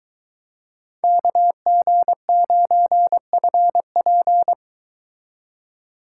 Morse code, heard as a keyed tone, about 22 wpm, 700 Hz. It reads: KG9FP